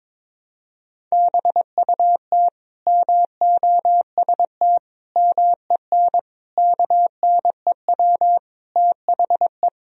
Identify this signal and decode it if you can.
Morse code, heard as a keyed tone, about 21 wpm, 700 Hz. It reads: BUT MOST MEN KNEW THE
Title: Sentences from Top 1000 Words - Review Part 1 - 22wpm